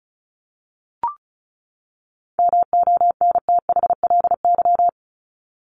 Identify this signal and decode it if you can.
Morse code, heard as a keyed tone, about 34 wpm, 700 Hz. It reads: MONTHLY
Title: LY Suffix - Top 100 Words with Base Words 35wpm